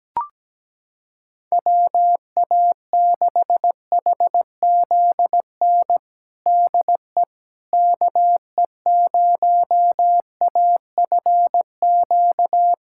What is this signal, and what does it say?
Morse code, heard as a keyed tone, about 17 wpm, 700 Hz. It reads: WA6HZN DE KE0AFQ
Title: USA Callsign DE Callsign - 6 Characters - 17wpm